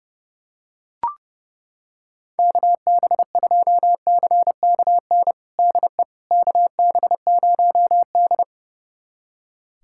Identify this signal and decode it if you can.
Morse code, heard as a keyed tone, about 30 wpm, 700 Hz. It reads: KB2CKN DE KB0D
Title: USA Callsign DE Callsign 30wpm